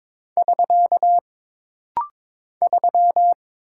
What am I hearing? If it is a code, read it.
Morse code, heard as a keyed tone, about 22 wpm, 700 Hz. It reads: <SK> 3
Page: This rapid-fire timing is geared to push your ICR (Instant Character Recognition) skills.